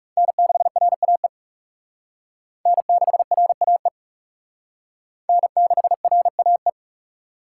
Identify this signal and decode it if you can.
Morse code, heard as a keyed tone, about 40 wpm, 700 Hz. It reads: N6RAE N6RAE N6RAE